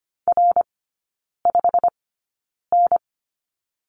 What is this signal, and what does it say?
Morse code, heard as a keyed tone, about 25 wpm, spaced as for 9 wpm, 700 Hz. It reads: R5N
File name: Lesson 15e - Number 5 - Three Characters 25wpm